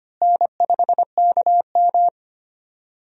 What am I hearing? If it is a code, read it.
Morse code, heard as a keyed tone, about 25 wpm, 700 Hz. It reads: N5KM
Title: USA Call Signs - 4 Characters - Repeated 3x - 25wpm